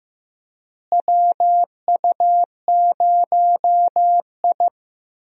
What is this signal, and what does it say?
Morse code, heard as a keyed tone, about 15 wpm, 700 Hz. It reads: WU0I
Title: USA Call Signs - 4 Characters - Repeated 3x - 15wpm